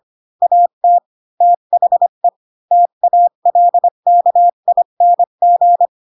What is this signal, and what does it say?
Morse code, heard as a keyed tone, about 25 wpm, 700 Hz. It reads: AT THE TALKING
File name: Pinocchio / Pinocchio ch4 20 25 ik1hge